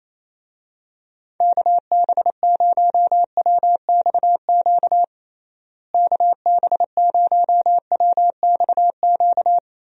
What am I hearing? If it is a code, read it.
Morse code, heard as a keyed tone, about 28 wpm, 700 Hz. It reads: KB0WXQ KB0WXQ
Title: USA Call Signs - 6 Characters - Repeated 2x - 28wpm